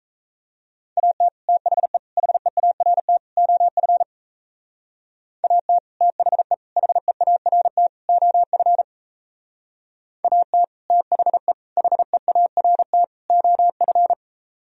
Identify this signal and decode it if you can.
Morse code, heard as a keyed tone, about 36 wpm, 700 Hz. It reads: AT THE HEART OF AT THE HEART OF AT THE HEART OF